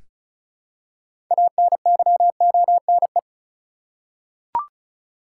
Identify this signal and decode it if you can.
Morse code, heard as a keyed tone, about 34 wpm, 700 Hz. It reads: ANYONE
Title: Six Letter Words - 35wpm